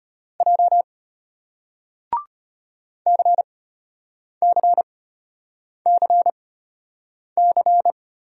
Morse code, heard as a keyed tone, about 33 wpm, spaced as for 8 wpm, 700 Hz. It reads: J CCCC